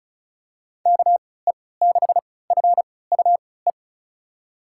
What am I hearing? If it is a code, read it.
Morse code, heard as a keyed tone, about 34 wpm, spaced as for 20 wpm, 690 Hz. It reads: KE6FUE
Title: USA Call Signs - 3x Character Spacing - 35wpm